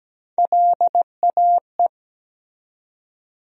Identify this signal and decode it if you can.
Morse code, heard as a keyed tone, about 17 wpm, 700 Hz. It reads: LAE
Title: USA Call Sign Postfixes - 3 Letters 17wpm